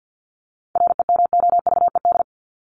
Morse code, heard as a keyed tone, about 50 wpm, 700 Hz. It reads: REMOVED
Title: ED Suffix - Top 100 Words with Base Words 50wpm